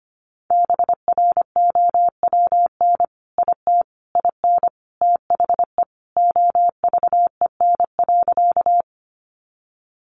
Morse code, heard as a keyed tone, about 25 wpm, 700 Hz. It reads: BROWN IT IN THE OVEN.
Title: Sentences from Top 1200 Words 25wpm